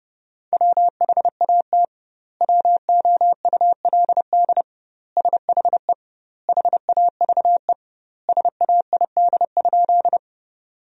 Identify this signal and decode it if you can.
Morse code, heard as a keyed tone, about 30 wpm, 700 Hz. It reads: WHAT WOULD SHE HAVE SAID?
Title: 2x Word-Spacing - Sentences from Top 100 Words - 30wpm